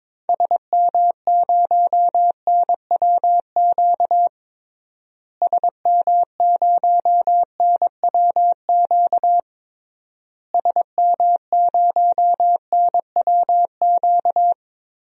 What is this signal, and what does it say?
Morse code, heard as a keyed tone, about 22 wpm, 700 Hz. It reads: SM0NWQ SM0NWQ SM0NWQ
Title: DXCC-100 Call Signs - 6 Characters - Repeated 3x - 22wpm